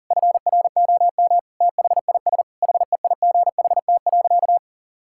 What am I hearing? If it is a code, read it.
Morse code, heard as a keyed tone, about 37 wpm, 700 Hz. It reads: FROM THIS HEIGHT.
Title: Sentences from Top 1600 Words 40wpm